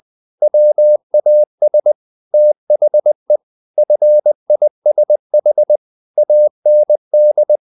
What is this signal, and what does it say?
Morse code, heard as a keyed tone, about 20 wpm, 600 Hz. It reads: WAS THE FISH AND
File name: the old man and the sea part 2 20wpm